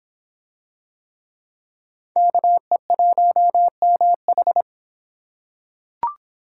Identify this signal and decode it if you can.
Morse code, heard as a keyed tone, about 26 wpm, 700 Hz. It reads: KE1MH